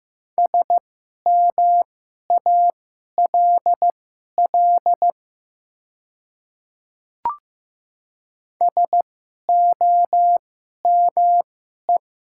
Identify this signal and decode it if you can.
Morse code, heard as a keyed tone, about 15 wpm, spaced as for 11 wpm, 700 Hz. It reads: SMALL SOME
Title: 2x Character Spacing - Top 300 Words - 15wpm